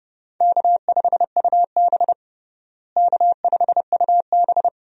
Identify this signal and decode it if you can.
Morse code, heard as a keyed tone, about 29 wpm, 700 Hz. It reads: K5UB K5UB
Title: USA Call Signs - 4 Characters - Repeated 2x - 30wpm